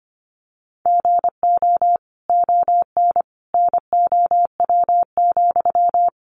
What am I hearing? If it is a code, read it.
Morse code, heard as a keyed tone, about 25 wpm, 700 Hz. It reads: GO ON NOW,
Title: Sentences from Top 1500 Words 25wpm